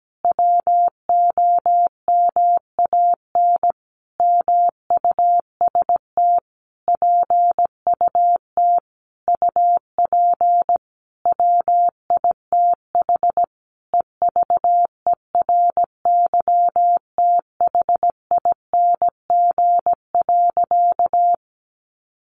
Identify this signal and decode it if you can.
Morse code, heard as a keyed tone, about 17 wpm, 700 Hz. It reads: WOMAN MUST PUT UP WITH EVERYTHING.
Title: Sentences from Top 1100 Words 17wpm